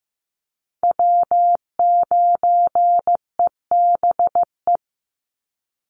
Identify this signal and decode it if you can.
Morse code, heard as a keyed tone, about 15 wpm, 700 Hz. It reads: W9EBE
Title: Call Signs Repeated 3x - 15wpm